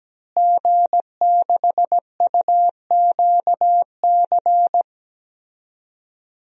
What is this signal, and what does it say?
Morse code, heard as a keyed tone, about 17 wpm, 700 Hz. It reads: G6UQC
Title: DXCC-100 Call Signs - 5 Characters - Repeated 2x - 17wpm